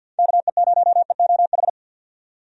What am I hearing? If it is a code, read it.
Morse code, heard as a keyed tone, about 50 wpm, 700 Hz. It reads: KE0EOH
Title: USA Call Signs - 6 Characters - 50wpm